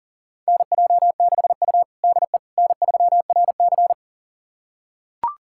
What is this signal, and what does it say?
Morse code, heard as a keyed tone, about 40 wpm, 700 Hz. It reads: NJ6U DE N3RC
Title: USA Callsign DE Callsign - 4 Characters - 40wpm